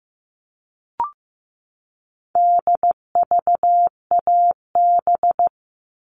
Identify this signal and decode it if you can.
Morse code, heard as a keyed tone, about 15 wpm, 700 Hz. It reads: DVAB
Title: All Cut Numbers - 4-Digit 15wpm